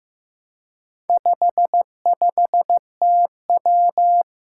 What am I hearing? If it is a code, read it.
Morse code, heard as a keyed tone, about 15 wpm, 700 Hz. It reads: 55TW
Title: Four Challenging Letter-Number Combinations 15wpm